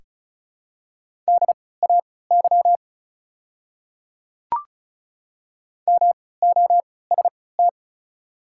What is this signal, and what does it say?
Morse code, heard as a keyed tone, about 36 wpm, spaced as for 20 wpm, 690 Hz. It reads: DAY MOST
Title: 3x Character Spacing - Top 100 Words - 35wpm